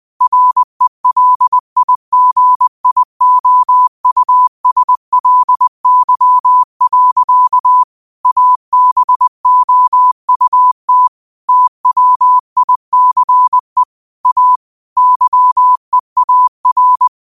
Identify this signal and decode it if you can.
Morse code, heard as a keyed tone, about 20 wpm, 1000 Hz. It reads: RELIGIOUSLY. ABOUT TWICE A YEAR
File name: qotdfiles / 20wpm